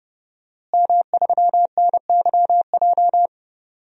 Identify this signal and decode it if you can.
Morse code, heard as a keyed tone, about 29 wpm, 700 Hz. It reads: M3NYJ
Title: DXCC-100 Call Signs - 5 Characters - Repeated 2x - 30wpm